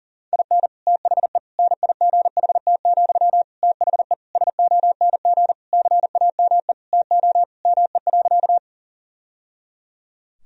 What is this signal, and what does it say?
Morse code, heard as a keyed tone, about 37 wpm, 700 Hz. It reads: IN THE NIGHT, THE SONG CAME TO ME.
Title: Sentences from Top 1000 Words - Review Part 2 - 40wpm